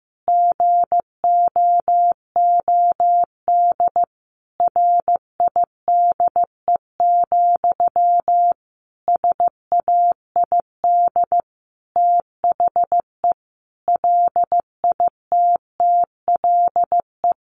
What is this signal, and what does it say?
Morse code, heard as a keyed tone, about 15 wpm, 700 Hz. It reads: GOOD RIDE, SAID THE LITTLE
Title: Sentences from Top 1200 Words 15wpm